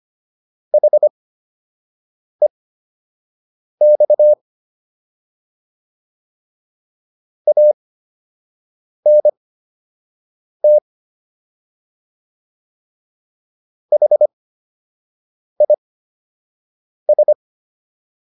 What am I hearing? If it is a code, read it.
Morse code, heard as a keyed tone, about 25 wpm, spaced as for 6 wpm, 600 Hz. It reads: HEX ANT HIS